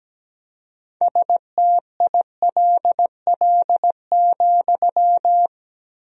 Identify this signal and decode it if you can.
Morse code, heard as a keyed tone, about 17 wpm, 700 Hz. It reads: STILL,
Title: 4x Word-Spacing - Sentences from Top 300 Words - 17wpm